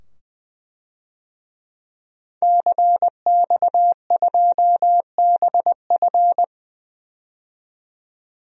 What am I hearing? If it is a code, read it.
Morse code, heard as a keyed tone, about 20 wpm, 700 Hz. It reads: CX2BF
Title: DXCC-100 Call Signs - 5 Characters - 20wpm